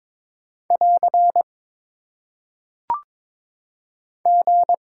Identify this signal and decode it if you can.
Morse code, heard as a keyed tone, about 22 wpm, 700 Hz. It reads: <AR> G
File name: Top 90 QSO Elements - 3x Character Spacing - 22wpm